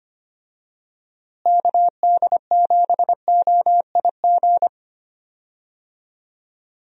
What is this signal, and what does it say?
Morse code, heard as a keyed tone, about 25 wpm, 700 Hz. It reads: KD7OIG
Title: USA Call Signs - 6 Characters - Repeated 3x - 25wpm